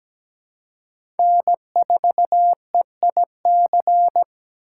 Morse code, heard as a keyed tone, about 17 wpm, 700 Hz. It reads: N4EIC
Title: USA Call Signs - 5 Characters - 3x Character Spacing - 17wpm